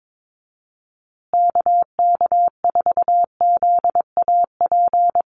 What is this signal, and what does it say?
Morse code, heard as a keyed tone, about 22 wpm, 700 Hz. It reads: KK4ZAP
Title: USA Call Signs Repeated 2x 22wpm